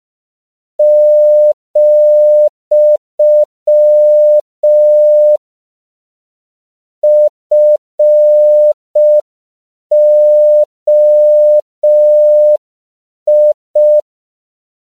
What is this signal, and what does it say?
Morse code, heard as a keyed tone, about 5 wpm, 600 Hz. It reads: , FOI
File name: CW / 5PPM